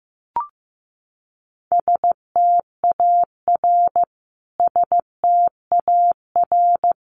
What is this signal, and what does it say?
Morse code, heard as a keyed tone, about 15 wpm, 700 Hz. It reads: STAR STAR
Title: Top 500 Words Repeated 15wpm